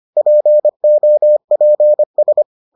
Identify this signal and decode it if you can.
Morse code, heard as a keyed tone, about 25 wpm, 600 Hz. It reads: POPS